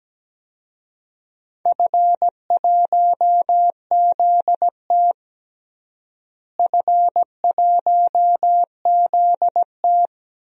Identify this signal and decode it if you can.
Morse code, heard as a keyed tone, about 17 wpm, 700 Hz. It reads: F1ZT F1ZT
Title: DXCC-100 Call Signs - 4 Characters - Repeated 2x - 17wpm